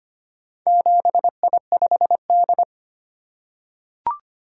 Morse code, heard as a keyed tone, about 25 wpm, 700 Hz. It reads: 7I5D
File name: practice / 25wpm / Four Letter-Number Combinations - 25wpm